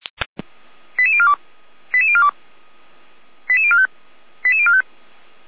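Five-tone selective call (ZVEI), heard as quick runs of five tones, tones of about 70 ms. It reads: 88042 88042 88045 88045
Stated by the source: ZVEI five-tone selective call received by SDR at feuerwehr-orlamuende, a fire brigade.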